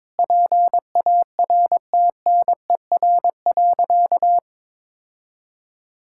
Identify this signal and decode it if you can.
Morse code, heard as a keyed tone, about 21 wpm, 700 Hz. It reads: PARTNER.
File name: practice / 22wpm / Sentences with 8-Word Phrases - 22wpm